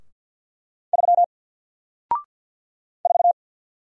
Morse code, heard as a keyed tone, about 48 wpm, 700 Hz. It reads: 3 4
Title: Single Character (Letter:Number:Punctuation:Prosign) Alt 50wpm